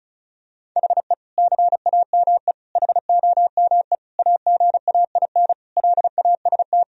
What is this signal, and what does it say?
Morse code, heard as a keyed tone, about 36 wpm, 700 Hz. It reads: HE CAME HOME AGAIN LAST
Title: Sentences from Top 1000 Words - Review Part 2 - 35wpm